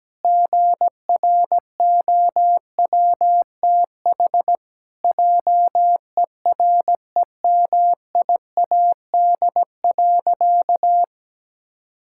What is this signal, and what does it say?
Morse code, heard as a keyed tone, about 17 wpm, 700 Hz. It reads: GROWTH JEREMIAD.
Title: Pangram Challenge v2 - 17wpm